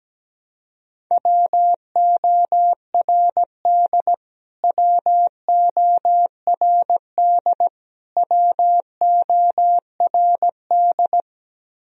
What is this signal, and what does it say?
Morse code, heard as a keyed tone, about 17 wpm, 700 Hz. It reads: WORD WORD WORD
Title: Top 100 Words Repeated 17wpm